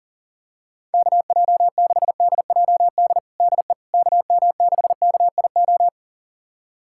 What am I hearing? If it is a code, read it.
Morse code, heard as a keyed tone, about 40 wpm, 700 Hz. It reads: KJ6DJD DE KM6KIO
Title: USA Callsign DE Callsign 40wpm